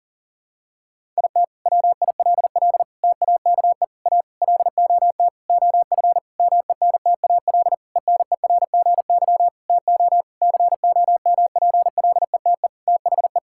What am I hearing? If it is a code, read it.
Morse code, heard as a keyed tone, about 40 wpm, 700 Hz. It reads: IT WILL TAKE A LOT OF MENTAL ENERGY TO COMPLETE THE
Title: Sentences from Top 1000 Words - Review Part 6 - 40wpm